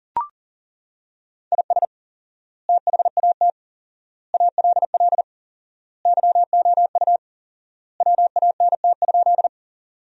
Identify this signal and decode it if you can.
Morse code, heard as a keyed tone, about 37 wpm, 700 Hz. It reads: IS THAT ALL YOU WANT?
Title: 4x Word-Spacing - Sentences from Top 200 Words - 40wpm